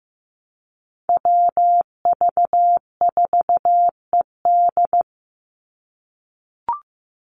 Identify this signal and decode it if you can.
Morse code, heard as a keyed tone, about 15 wpm, 700 Hz. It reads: WV4ED
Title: USA Call Signs 15wpm